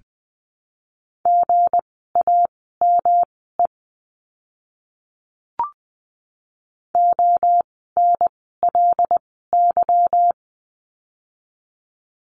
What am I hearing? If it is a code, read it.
Morse code, heard as a keyed tone, about 20 wpm, spaced as for 14 wpm, 700 Hz. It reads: GAME ONLY
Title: 160 Lesson 26f - Letter g - Words 20wpm